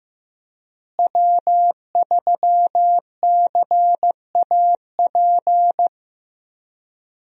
Morse code, heard as a keyed tone, about 16 wpm, 700 Hz. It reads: W3CAP
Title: USA Call Signs - 5 Characters - Repeated 3x - 15wpm